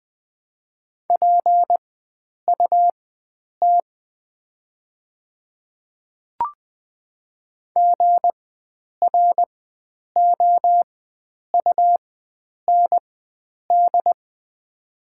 Morse code, heard as a keyed tone, about 20 wpm, spaced as for 9 wpm, 690 Hz. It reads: PUT GROUND
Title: 4x Character Spacing - Top 500 Words - 20wpm